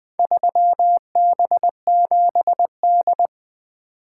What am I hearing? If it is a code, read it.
Morse code, heard as a keyed tone, about 20 wpm, 700 Hz. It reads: 3B7D
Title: Four Letter-Number Combinations - 20wpm